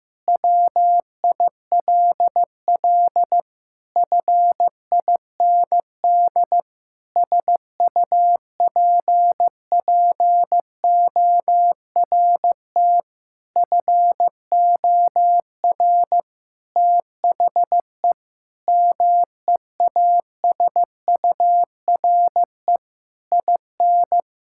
Morse code, heard as a keyed tone, about 15 wpm, 700 Hz. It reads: WILL FIND SUPPORT FOR THE MEASURE IN
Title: Sentences from Top 1000 Words - Review Part 5 - 15wpm